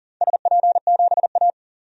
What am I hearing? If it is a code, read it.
Morse code, heard as a keyed tone, about 40 wpm, 700 Hz. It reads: SP7A
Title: DXCC-100 Call Signs - 4 Characters - Repeated 2x - 40wpm